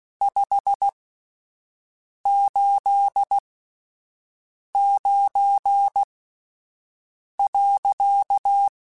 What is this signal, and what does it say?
Morse code, heard as a keyed tone, about 16 wpm, spaced as for 5 wpm, 800 Hz. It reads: 589.
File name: QSOs / QSOs-mp3s / WD0FBT 1@10